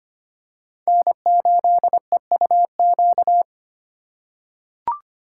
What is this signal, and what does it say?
Morse code, heard as a keyed tone, about 25 wpm, 700 Hz. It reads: N8EUQ
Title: USA Call Signs - 5 Characters - 3x Character Spacing - 25wpm